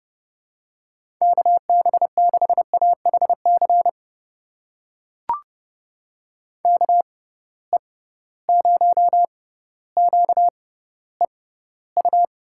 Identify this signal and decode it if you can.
Morse code, heard as a keyed tone, about 32 wpm, spaced as for 16 wpm, 700 Hz. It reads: KB6AHC KE0QEU
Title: USA Call Signs - 6 Characters - 6x Character Spacing - 30wpm